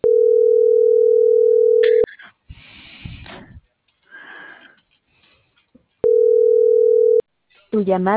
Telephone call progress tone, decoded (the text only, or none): ringback tone